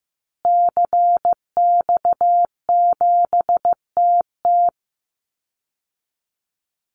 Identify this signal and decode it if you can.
Morse code, heard as a keyed tone, about 15 wpm, 700 Hz. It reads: CX7TT
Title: Call Signs Repeated 3x - 15wpm